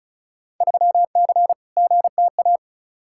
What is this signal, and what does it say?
Morse code, heard as a keyed tone, about 36 wpm, 700 Hz. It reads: 3C GTA
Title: Field Day Exchanges Encore - 35wpm